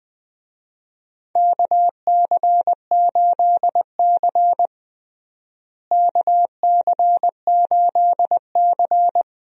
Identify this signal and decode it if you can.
Morse code, heard as a keyed tone, about 20 wpm, 700 Hz. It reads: KC8C KC8C
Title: USA Call Signs - 4 Characters - Repeated 3x - 20wpm